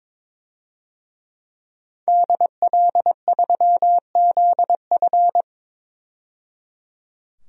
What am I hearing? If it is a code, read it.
Morse code, heard as a keyed tone, about 22 wpm, 700 Hz. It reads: DL3ZF
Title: DXCC-100 Call Signs - 5 Characters - 22wpm